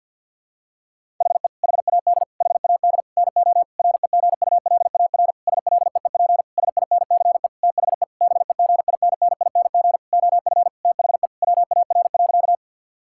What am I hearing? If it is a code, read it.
Morse code, heard as a keyed tone, about 49 wpm, 700 Hz. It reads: HE HAD HAD NO REGULAR SLEEP SINCE THE BEGINNING OF THE WAR.
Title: Sentences from Top 1000 Words - Review Part 6 - 50wpm